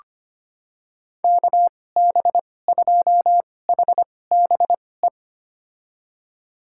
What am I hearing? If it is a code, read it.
Morse code, heard as a keyed tone, about 25 wpm, spaced as for 18 wpm, 700 Hz. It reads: KB2HBE
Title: USA Call Signs - 6 Characters - 2x Character Spacing - 25wpm